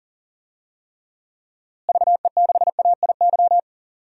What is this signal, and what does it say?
Morse code, heard as a keyed tone, about 40 wpm, 700 Hz. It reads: VE6AIY
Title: DXCC-100 Call Signs - 6 Characters - Repeated 3x - 40wpm